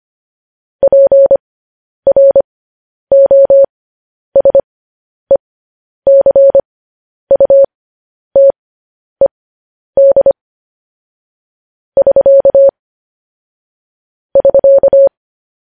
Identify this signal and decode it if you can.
Morse code, heard as a keyed tone, about 25 wpm, spaced as for 10 wpm, 550 Hz. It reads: PROSECUTED <SK> <SK>